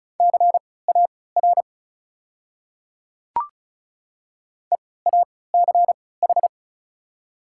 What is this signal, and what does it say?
Morse code, heard as a keyed tone, about 34 wpm, spaced as for 20 wpm, 700 Hz. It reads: CAR EACH
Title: 3x Character Spacing - Top 300 Words - 35wpm